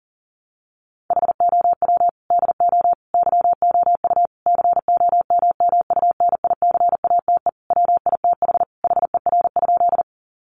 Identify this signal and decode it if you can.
Morse code, heard as a keyed tone, about 40 wpm, 700 Hz. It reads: HOW DO YOU COMMUNICATE WITH HER?
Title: Sentences from Top 1300 Words 40wpm